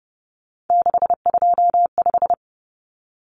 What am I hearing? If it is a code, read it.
Morse code, heard as a keyed tone, about 30 wpm, 700 Hz. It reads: 625
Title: Numbers - 3-Digit 30wpm